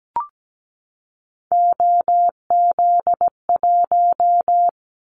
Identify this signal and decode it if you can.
Morse code, heard as a keyed tone, about 17 wpm, 700 Hz. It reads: OZ1
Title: Callsign Prefixes for Top 100 DXCC - 17wpm